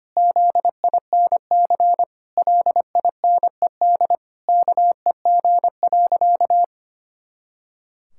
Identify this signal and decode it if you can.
Morse code, heard as a keyed tone, about 25 wpm, 700 Hz. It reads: ZINC LINED KEG.
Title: Pangram Challenge v2 - 25wpm